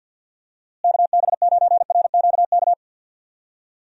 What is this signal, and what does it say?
Morse code, heard as a keyed tone, about 52 wpm, 700 Hz. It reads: KB9RQK
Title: USA Call Signs - 6 Characters - Repeated 2x - 50wpm